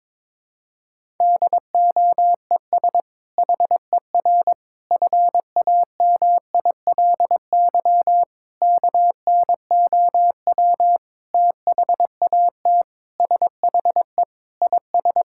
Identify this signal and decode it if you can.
Morse code, heard as a keyed tone, about 22 wpm, 700 Hz. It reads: DOES HER FAMILY KNOW THAT SHE IS
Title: Sentences from Top 1000 Words - Review Part 6 - 22wpm